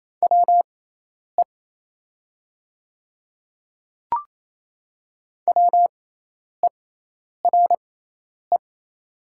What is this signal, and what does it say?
Morse code, heard as a keyed tone, about 27 wpm, spaced as for 10 wpm, 700 Hz. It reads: WE WERE
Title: 6x Character Spacing - Top 100 Words - 28wpm